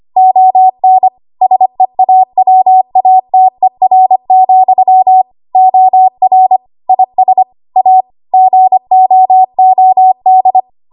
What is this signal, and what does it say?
Morse code, heard as a keyed tone, about 25 wpm, 750 Hz. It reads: ON SEAWATER, OR IS A GOOD
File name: Archive / 25 WPM / 25WPM